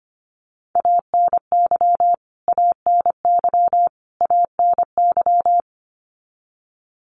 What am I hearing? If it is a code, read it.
Morse code, heard as a keyed tone, about 25 wpm, 700 Hz. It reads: ANY ANY ANY
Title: Top 500 Words Repeated 25wpm